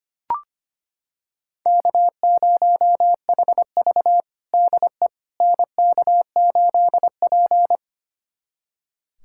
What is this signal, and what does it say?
Morse code, heard as a keyed tone, about 25 wpm, 700 Hz. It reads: K0HV DE NK8P
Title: USA Callsign DE Callsign - 4 Characters - 25wpm